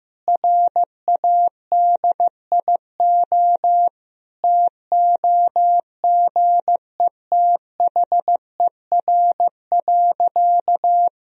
Morse code, heard as a keyed tone, about 16 wpm, 700 Hz. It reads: RADIO TOGETHER.
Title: Sentences from Top 1000 Words - Review Part 3 - 15wpm